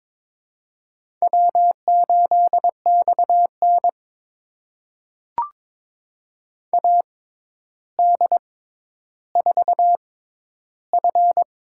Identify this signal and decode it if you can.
Morse code, heard as a keyed tone, about 22 wpm, spaced as for 11 wpm, 700 Hz. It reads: W8XN AD4F
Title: USA Call Signs - 4 Characters - 6x Character Spacing - 22wpm